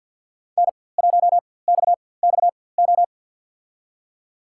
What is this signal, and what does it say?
Morse code, heard as a keyed tone, about 53 wpm, spaced as for 24 wpm, 700 Hz. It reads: N1XXO
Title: USA Call Signs - 5 Characters - 4x Character Spacing - 50wpm